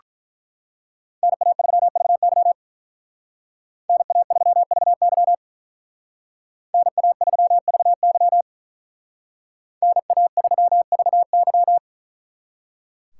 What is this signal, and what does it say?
Morse code, heard as a keyed tone, about 42 wpm, 700 Hz. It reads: NA3VY NA3VY NA3VY NA3VY
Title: Speed Racing - USA 2x2 Call Signs 35wpm